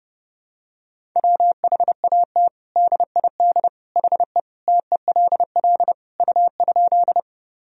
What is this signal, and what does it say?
Morse code, heard as a keyed tone, about 30 wpm, 700 Hz. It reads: WHAT DID HE TELL U?
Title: Sentences from Top 200 Words and Abbreviations 30wpm